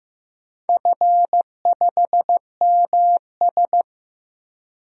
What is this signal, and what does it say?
Morse code, heard as a keyed tone, about 15 wpm, 700 Hz. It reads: F5MS
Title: DXCC-100 Call Signs - 4 Characters - 15wpm